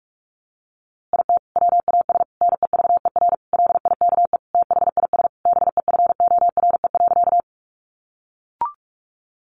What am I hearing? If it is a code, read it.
Morse code, heard as a keyed tone, about 46 wpm, 700 Hz. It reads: IT WAS NEVER LIKE THIS BEFORE.
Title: A Sound Of Thunder 45wpm